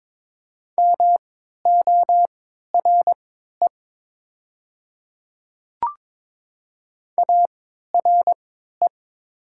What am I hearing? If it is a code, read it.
Morse code, heard as a keyed tone, about 22 wpm, spaced as for 13 wpm, 700 Hz. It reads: MORE ARE
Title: 3x Character Spacing - Top 100 Words - 22wpm